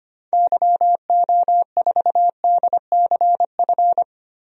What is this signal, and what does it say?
Morse code, heard as a keyed tone, about 25 wpm, 700 Hz. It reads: YO4DCF
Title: DXCC-100 Call Signs - 6 Characters - Repeated 2x - 25wpm